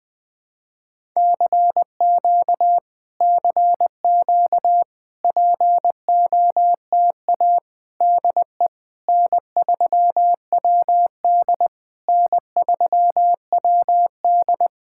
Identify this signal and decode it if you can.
Morse code, heard as a keyed tone, about 20 wpm, 700 Hz. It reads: CQ CQ POTA DE N3WD N3WD
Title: POTA QSO Practice - 20wpm